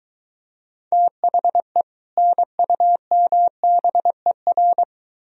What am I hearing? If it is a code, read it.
Morse code, heard as a keyed tone, about 22 wpm, 700 Hz. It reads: THE NUMBER